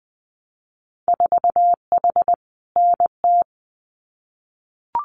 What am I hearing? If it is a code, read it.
Morse code, heard as a keyed tone, about 19 wpm, 700 Hz. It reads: 4H NT
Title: Winter Field Day Exchanges Encore 20wpm